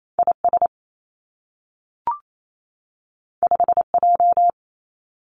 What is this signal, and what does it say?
Morse code, heard as a keyed tone, about 28 wpm, 700 Hz. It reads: I S 5 J
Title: Two Challenging Letter-Number Combinations 28wpm